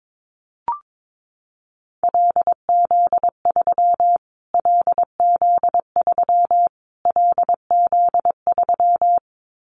Morse code, heard as a keyed tone, about 22 wpm, 700 Hz. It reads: LZ3 LZ3 LZ3
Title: Callsign Prefixes for Top 25 DXCC - Repeated 3x - 22wpm